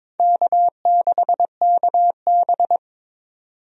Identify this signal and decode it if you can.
Morse code, heard as a keyed tone, about 22 wpm, 700 Hz. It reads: K6KB
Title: USA Call Signs - 4 Characters - 22wpm